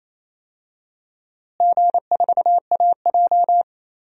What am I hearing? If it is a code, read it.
Morse code, heard as a keyed tone, about 28 wpm, 700 Hz. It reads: G4AJ
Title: DXCC-100 Call Signs - 4 Characters - Repeated 2x - 28wpm